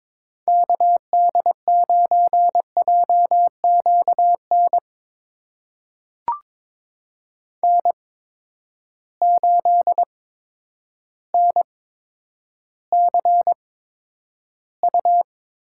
Morse code, heard as a keyed tone, about 22 wpm, spaced as for 10 wpm, 700 Hz. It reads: KD9JQN N8NCU
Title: USA Call Signs - 8x Character Spacing - 22wpm